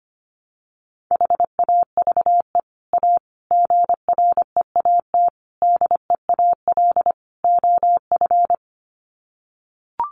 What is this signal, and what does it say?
Morse code, heard as a keyed tone, about 25 wpm, 700 Hz. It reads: HAVE A GREAT DEAL OF
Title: Sets of 5 Words Encore2 25wpm